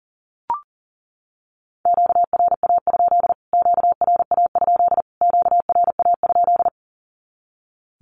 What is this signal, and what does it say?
Morse code, heard as a keyed tone, about 40 wpm, 700 Hz. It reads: QRA? QRA? QRA?
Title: Q-Codes Repeated 3x 40wpm